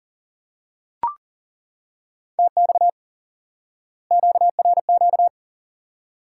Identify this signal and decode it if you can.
Morse code, heard as a keyed tone, about 40 wpm, 700 Hz. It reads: TX QRQ
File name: IQE Course 3 - Lesson 89 - YAGI - Two Elements - 40wpm